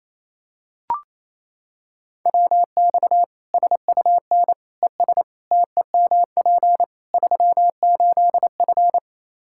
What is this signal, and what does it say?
Morse code, heard as a keyed tone, about 28 wpm, 700 Hz. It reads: WX SUN ES TEMP 38F
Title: Practice QSO - Exchange 2 - Info - 28wpm